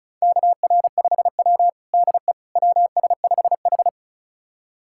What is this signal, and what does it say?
Morse code, heard as a keyed tone, about 36 wpm, 700 Hz. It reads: KR5W DE WS5H
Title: USA Callsign DE Callsign - 4 Characters - 35wpm